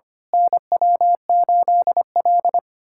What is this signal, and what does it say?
Morse code, heard as a keyed tone, about 25 wpm, 700 Hz. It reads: NW8L
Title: USA Callsign DE Callsign - 4 Characters - 25wpm